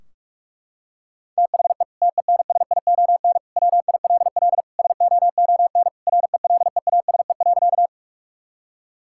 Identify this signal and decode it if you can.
Morse code, heard as a keyed tone, about 44 wpm, 700 Hz. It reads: THE TENSION WILL SOON RELEASE.
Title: Sentences from Top 1600 Words 45wpm